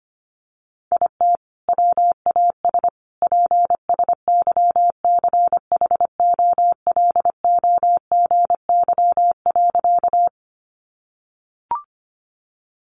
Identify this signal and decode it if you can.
Morse code, heard as a keyed tone, about 25 wpm, 700 Hz. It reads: IT WAS PSYCHOLOGY.
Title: Sentences from Top 1400 Words 25wpm